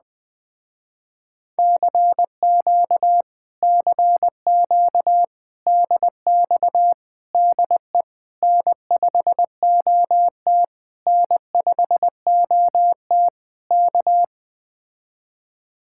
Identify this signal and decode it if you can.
Morse code, heard as a keyed tone, about 20 wpm, 700 Hz. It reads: CQ CQ DX DE N5OT N5OT K
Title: Speed-Racing Common QSO Phrases - 17wpm